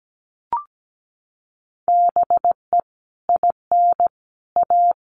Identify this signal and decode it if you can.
Morse code, heard as a keyed tone, about 17 wpm, 700 Hz. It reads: BE IN A
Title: Sets of 5 Words Encore2 17wpm